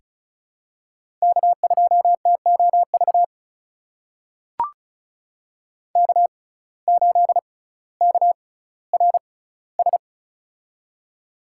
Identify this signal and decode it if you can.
Morse code, heard as a keyed tone, about 36 wpm, spaced as for 18 wpm, 700 Hz. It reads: K2TOV K8KRS
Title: USA Call Signs - 5 Characters - 6x Character Spacing - 35wpm